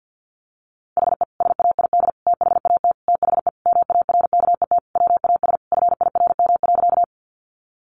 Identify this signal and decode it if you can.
Morse code, heard as a keyed tone, about 50 wpm, 700 Hz. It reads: HE SAID THAT THE MARKET WAS FIRM.
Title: Sentences from Top 1400 Words 50wpm